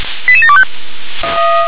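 Five-tone selective call (ZVEI), heard as quick runs of five tones, tones of about 70 ms. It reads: 88725
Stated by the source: ZVEI five-tone selective call received by SDR.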